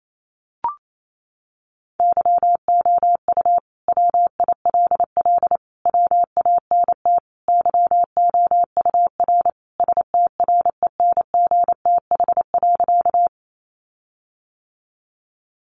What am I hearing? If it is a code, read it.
Morse code, heard as a keyed tone, about 28 wpm, 700 Hz. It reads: YOU WILL WANT YOUR STRENGTH.
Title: Sentences from Top 1300 Words 28wpm